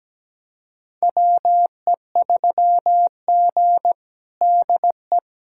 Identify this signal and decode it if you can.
Morse code, heard as a keyed tone, about 17 wpm, 700 Hz. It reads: WE3G DE
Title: USA Callsign DE Callsign - 4 Characters - 17wpm